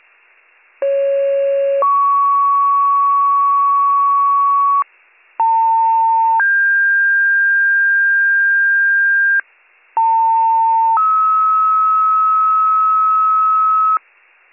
Two-tone page, heard as a one-second tone, then a three-second tone, about 550 then 1120 Hz, about 900 then 1600 Hz, about 900 then 1290 Hz, with noise. It is heard from 0.8 to 14.0 s.